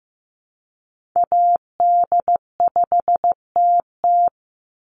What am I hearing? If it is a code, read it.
Morse code, heard as a keyed tone, about 15 wpm, 700 Hz. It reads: AD5TT
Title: Call Signs Repeated 3x - 15wpm